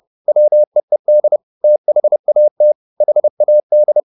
Morse code, heard as a keyed tone, about 30 wpm, 600 Hz. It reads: WEED THAT HAD
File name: the old man and the sea / the old man and the sea 1 30wpm